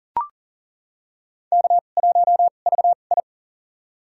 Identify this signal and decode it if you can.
Morse code, heard as a keyed tone, about 40 wpm, spaced as for 29 wpm, 700 Hz. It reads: K1VI
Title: USA Call Signs - 4 Characters - 2x Character Spacing - 40wpm